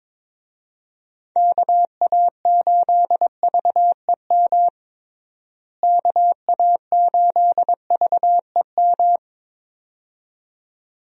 Morse code, heard as a keyed tone, about 22 wpm, 700 Hz. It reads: KA8VEM KA8VEM
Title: USA Call Signs - 6 Characters - Repeated 2x - 22wpm